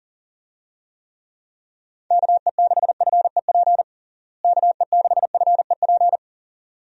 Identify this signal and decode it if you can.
Morse code, heard as a keyed tone, about 37 wpm, 700 Hz. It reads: KE6FEP KE6FEP
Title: USA Call Signs - 6 Characters - Repeated 2x - 40wpm